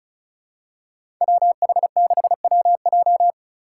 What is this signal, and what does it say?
Morse code, heard as a keyed tone, about 36 wpm, 700 Hz. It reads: WH6WJ
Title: USA Call Signs - 5 Characters - 4x Character Spacing - 35wpm